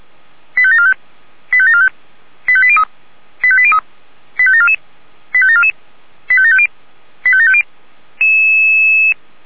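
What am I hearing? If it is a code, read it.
Five-tone selective call (ZVEI), heard as quick runs of five tones, tones of about 70 ms. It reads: 85745 85745 85803 85803 85744 85744 85750 85750
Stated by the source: ZVEI five-tone selective call received on SDR.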